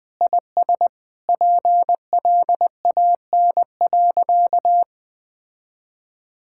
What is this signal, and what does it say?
Morse code, heard as a keyed tone, about 20 wpm, 700 Hz. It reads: IS PLAN.
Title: Sentences with 8-Word Phrases - 20wpm